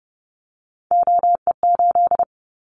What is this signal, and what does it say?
Morse code, heard as a keyed tone, about 30 wpm, 700 Hz. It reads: OE8
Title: Callsign Prefixes for Top 100 DXCC - 30wpm